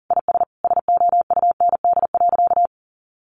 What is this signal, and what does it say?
Morse code, heard as a keyed tone, about 40 wpm, 700 Hz. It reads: IS SOUND.
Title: Sentences from Top 1400 Words 40wpm